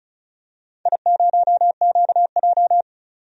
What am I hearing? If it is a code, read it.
Morse code, heard as a keyed tone, about 36 wpm, 690 Hz. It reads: I0QJ
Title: DXCC-100 Call Signs - 4 Characters - 35wpm